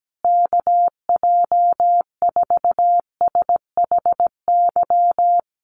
Morse code, heard as a keyed tone, about 17 wpm, 700 Hz. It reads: KJ4SHY
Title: USA Call Signs Repeated 3x 17wpm